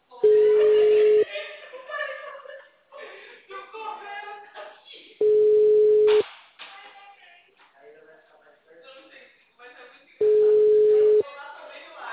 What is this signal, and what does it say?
Ringback tone (Europe, 425 Hz), with speech.
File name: out- -2004- - - .55831